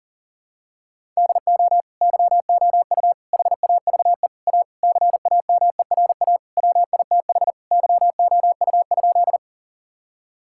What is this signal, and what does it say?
Morse code, heard as a keyed tone, about 40 wpm, 700 Hz. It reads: DO YOU HAVE A CAMERA WITH YOU?
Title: Sentences from Top 1000 Words - Review Part 5 - 40wpm